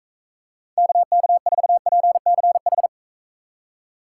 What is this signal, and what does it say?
Morse code, heard as a keyed tone, about 45 wpm, 700 Hz. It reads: KK4PCH